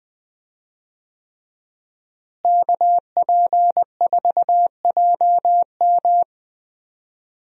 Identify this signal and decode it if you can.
Morse code, heard as a keyed tone, about 20 wpm, 700 Hz. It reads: KP4JM
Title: DXCC-100 Call Signs - 5 Characters - Repeated 3x - 20wpm